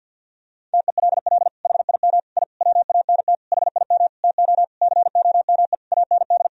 Morse code, heard as a keyed tone, about 50 wpm, 700 Hz. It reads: TELL HIM I WANT HIM TO COME AND
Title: Sentences from Top 1000 Words - Review Part 2 - 50wpm